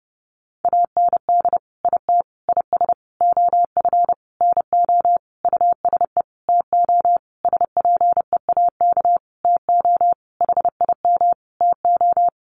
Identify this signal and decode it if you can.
Morse code, heard as a keyed tone, about 30 wpm, 700 Hz. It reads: AND IT IS OF NO USE TO SPEAK TO HIM TO
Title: Sentences from Top 1100 Words 30wpm